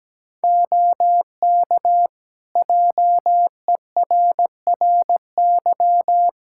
Morse code, heard as a keyed tone, about 17 wpm, 700 Hz. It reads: OK JERRY
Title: Practice QSO - Exchange 3 - More Info - 17wpm